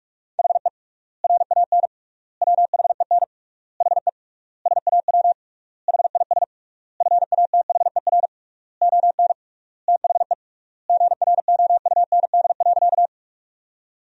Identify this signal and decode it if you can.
Morse code, heard as a keyed tone, about 46 wpm, 700 Hz. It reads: HE RAN WHEN HE SAW HIS FATHER ON THE GROUND.
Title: 3x Word-Spacing - Sentences from Top 500 Words - 45wpm